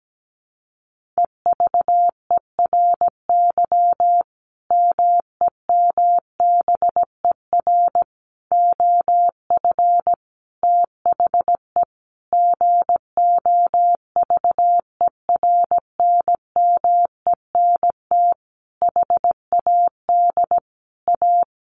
Morse code, heard as a keyed tone, about 17 wpm, 700 Hz. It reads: EVERY MEMBER OF THE GOVERNMENT HAD A
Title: Sentences from Top 1400 Words 17wpm